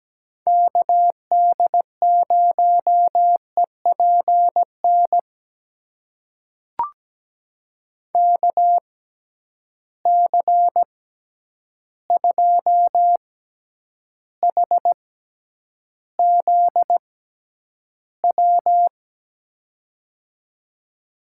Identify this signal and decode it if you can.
Morse code, heard as a keyed tone, about 17 wpm, spaced as for 9 wpm, 700 Hz. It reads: KD0EPN KC2HZW
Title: USA Call Signs - 6 Characters - 6x Character Spacing - 17wpm